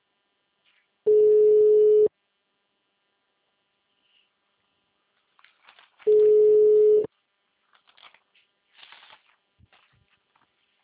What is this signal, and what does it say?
Ringback tone (Europe, 425 Hz).